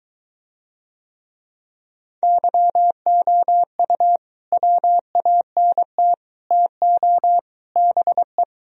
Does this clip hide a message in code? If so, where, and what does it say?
Yes: Morse code from 2.2 to 8.4 s, heard as a keyed tone, about 23 wpm, 700 Hz. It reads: YOU WANT TO BE